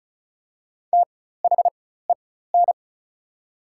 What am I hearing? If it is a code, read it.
Morse code, heard as a keyed tone, about 36 wpm, spaced as for 16 wpm, 700 Hz. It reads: THEN
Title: 4x Character Spacing - Top 100 Words - 35wpm